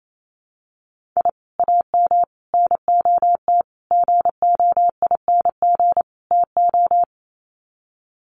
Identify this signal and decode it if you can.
Morse code, heard as a keyed tone, about 28 wpm, 690 Hz. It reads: I AM NOT GOING TO
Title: Sets of 5 Words Encore 28wpm